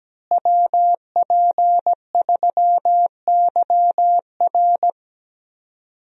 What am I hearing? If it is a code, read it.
Morse code, heard as a keyed tone, about 17 wpm, 700 Hz. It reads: WP3YR
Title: USA Call Signs - 5 Characters - Repeated 2x - 17wpm